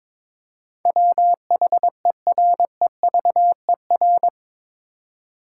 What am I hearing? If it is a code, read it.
Morse code, heard as a keyed tone, about 22 wpm, 700 Hz. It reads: WHEREVER
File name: practice / 22wpm / Eight Letter Words - 22wpm